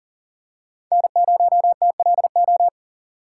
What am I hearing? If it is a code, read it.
Morse code, heard as a keyed tone, about 40 wpm, 700 Hz. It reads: N0TLO
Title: USA Call Signs - 6x Character Spacing - 40wpm